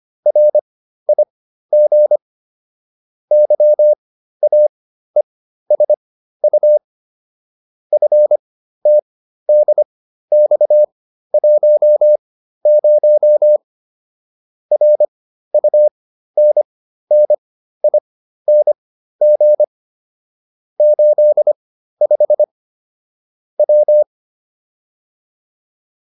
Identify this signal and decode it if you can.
Morse code, heard as a keyed tone, about 25 wpm, spaced as for 13 wpm, 600 Hz. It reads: RIG YAESU FTDX10 RUNNING 85 W